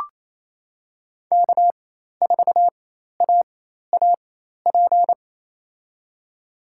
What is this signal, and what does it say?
Morse code, heard as a keyed tone, about 28 wpm, spaced as for 13 wpm, 700 Hz. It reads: K4AAP
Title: USA Call Signs - 4x Character Spacing - 28wpm